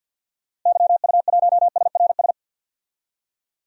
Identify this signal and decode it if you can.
Morse code, heard as a keyed tone, about 47 wpm, 700 Hz. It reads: YU1SRS
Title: DXCC-100 Call Signs - 6 Characters - Repeated 2x - 50wpm